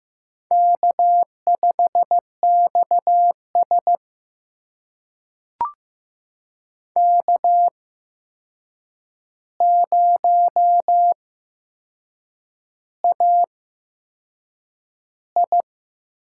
Morse code, heard as a keyed tone, about 15 wpm, spaced as for 6 wpm, 700 Hz. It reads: K5XS K0AI
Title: USA Call Signs - 4 Characters - 8x Character Spacing - 15wpm